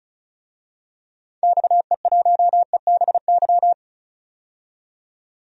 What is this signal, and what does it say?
Morse code, heard as a keyed tone, about 36 wpm, 700 Hz. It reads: XE1EBY
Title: DXCC-100 Call Signs - 6 Characters - 35wpm